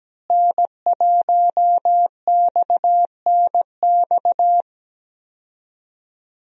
Morse code, heard as a keyed tone, about 17 wpm, 700 Hz. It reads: N1XNX
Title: USA Call Signs - 5 Characters - Repeated 3x - 17wpm